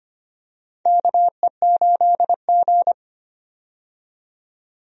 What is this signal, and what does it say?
Morse code, heard as a keyed tone, about 25 wpm, 700 Hz. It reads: KE8G
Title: USA Call Signs - 4 Characters - 25wpm